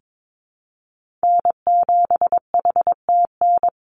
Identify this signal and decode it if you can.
Morse code, heard as a keyed tone, about 22 wpm, 700 Hz. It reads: N7HTN
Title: USA Call Signs 22wpm